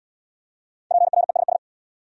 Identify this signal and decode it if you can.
Morse code, heard as a keyed tone, about 69 wpm, 700 Hz. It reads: PRESS